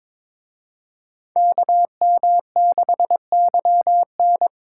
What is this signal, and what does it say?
Morse code, heard as a keyed tone, about 26 wpm, 700 Hz. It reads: KM6YN
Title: Speed Racing - USA 2x2 Call Signs 22wpm